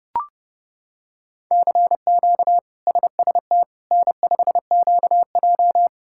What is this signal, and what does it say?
Morse code, heard as a keyed tone, about 30 wpm, 700 Hz. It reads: CQ SST N5QJ
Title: K1USN SST QSO Practice - 30wpm